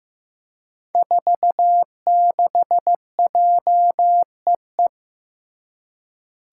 Morse code, heard as a keyed tone, about 15 wpm, 700 Hz. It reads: 46JEE
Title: Five Challenging Letter-Number Combinations 15wpm